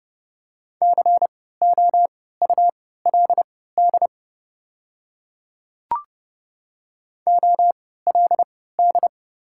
Morse code, heard as a keyed tone, about 30 wpm, spaced as for 17 wpm, 700 Hz. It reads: COULD OLD
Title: 3x Character Spacing - Top 200 Words - 30wpm